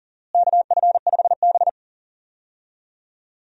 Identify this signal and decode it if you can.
Morse code, heard as a keyed tone, about 40 wpm, 700 Hz. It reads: KF5B
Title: USA Call Signs - 4 Characters - Repeated 2x - 40wpm